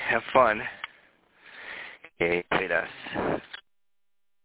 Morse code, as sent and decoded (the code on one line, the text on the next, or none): none